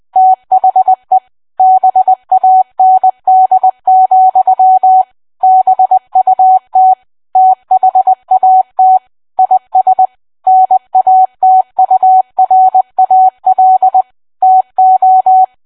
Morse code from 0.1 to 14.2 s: THE BAND, BUT THAT IS NATURAL